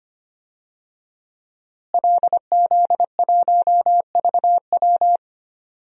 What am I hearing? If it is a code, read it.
Morse code, heard as a keyed tone, about 25 wpm, 700 Hz. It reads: LZ1VW
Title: DXCC-100 Call Signs - 5 Characters - Repeated 3x - 25wpm